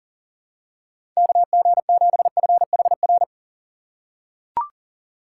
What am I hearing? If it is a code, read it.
Morse code, heard as a keyed tone, about 40 wpm, 700 Hz. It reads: KG7FHR
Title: USA Call Signs - 6 Characters - 4x Character Spacing - 40wpm